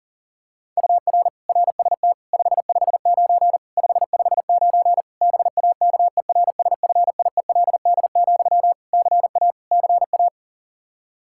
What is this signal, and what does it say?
Morse code, heard as a keyed tone, about 40 wpm, 700 Hz. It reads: UR RST 559 559 BAKERSFIELD, CA CA
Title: K1USN SST QSO Practice 40wpm